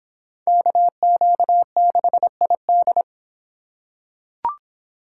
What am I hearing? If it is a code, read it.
Morse code, heard as a keyed tone, about 26 wpm, 700 Hz. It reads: KQ6ID